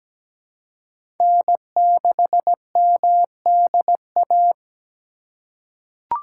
Morse code, heard as a keyed tone, about 17 wpm, 700 Hz. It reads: N6MDA
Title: USA Call Signs - 2x Character Spacing - 17wpm